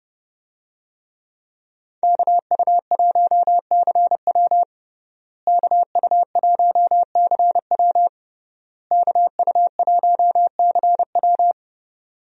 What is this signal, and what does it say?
Morse code, heard as a keyed tone, about 30 wpm, 700 Hz. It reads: KU1CW KU1CW KU1CW
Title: USA Call Signs - 5 Characters - Repeated 3x - 30wpm